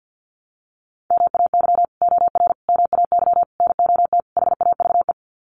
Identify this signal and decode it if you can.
Morse code, heard as a keyed tone, about 48 wpm, 700 Hz. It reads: MAY OR MAY NOT HAVE
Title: Sets of 5 Words Encore2 50wpm